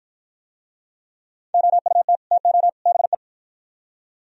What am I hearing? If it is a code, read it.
Morse code, heard as a keyed tone, about 49 wpm, 700 Hz. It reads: OUT TO BE